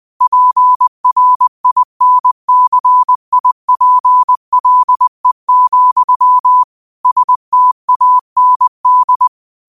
Morse code, heard as a keyed tone, about 20 wpm, 1000 Hz. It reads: PRINCIPLE, STAND